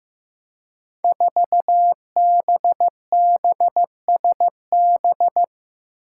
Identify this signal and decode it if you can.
Morse code, heard as a keyed tone, about 15 wpm, 700 Hz. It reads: 4BBSB
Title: Five Challenging Letter-Number Combinations 15wpm